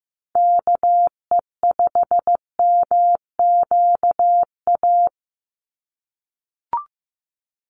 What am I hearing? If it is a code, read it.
Morse code, heard as a keyed tone, about 15 wpm, 700 Hz. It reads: KE5MQA
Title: USA Call Signs 15wpm